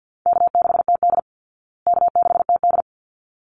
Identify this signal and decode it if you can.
Morse code, heard as a keyed tone, about 48 wpm, 700 Hz. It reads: K6TD K6TD
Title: Call Signs Repeated 3x - 50wpm